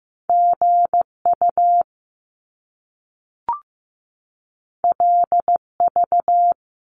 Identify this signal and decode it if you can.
Morse code, heard as a keyed tone, about 15 wpm, 700 Hz. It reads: GU LV
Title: Top Two Letter Combinations in Words 15wpm